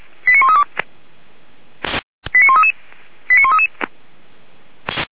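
Five-tone selective call (ZVEI), heard as quick runs of five tones, tones of about 70 ms. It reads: 89132 89133 89133
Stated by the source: ZVEI five-tone selective call received with an SDR.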